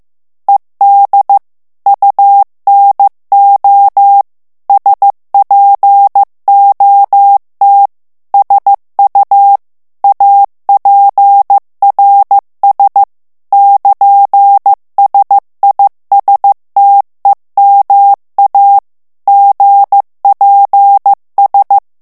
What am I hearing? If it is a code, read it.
Morse code, heard as a keyed tone, about 15 wpm, 800 Hz. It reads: ED UNO SPOT SU APRS <KN>SISTEMA GPS
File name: ITALIEN texte clair 1 vitesse 14 mots minute